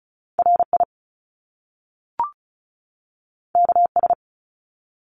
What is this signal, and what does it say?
Morse code, heard as a keyed tone, about 36 wpm, 700 Hz. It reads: RI KS
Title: US States 35wpm